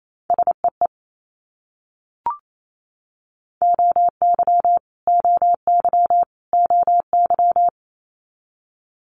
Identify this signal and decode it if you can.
Morse code, heard as a keyed tone, about 28 wpm, 700 Hz. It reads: SEE OY OY OY
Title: Multi-letter Phonemes Repeated 3x 28wpm